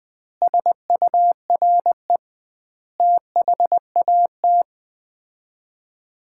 Morse code, heard as a keyed tone, about 20 wpm, 700 Hz. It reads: SURE THAT
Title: 2x Word-Spacing - Sets of 5 Words - 20wpm